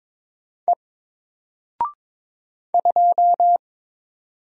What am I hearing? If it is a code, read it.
Morse code, heard as a keyed tone, about 22 wpm, 700 Hz. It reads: E 2
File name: Single-Letter-Number-22wpm